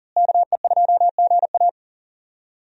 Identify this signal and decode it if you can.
Morse code, heard as a keyed tone, about 40 wpm, 700 Hz. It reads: KE2GA
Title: USA Call Signs - 5 Characters - Repeated 3x - 40wpm